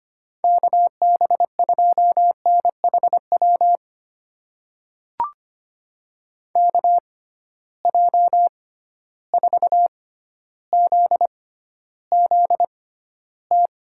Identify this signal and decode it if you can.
Morse code, heard as a keyed tone, about 25 wpm, spaced as for 13 wpm, 700 Hz. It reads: KB2NHW KJ4ZZT
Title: USA Call Signs - 6 Characters - 6x Character Spacing - 25wpm